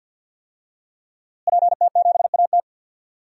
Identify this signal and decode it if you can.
Morse code, heard as a keyed tone, about 45 wpm, 700 Hz. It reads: PT7AT
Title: DXCC-100 Call Signs - 5 Characters - Repeated 2x - 50wpm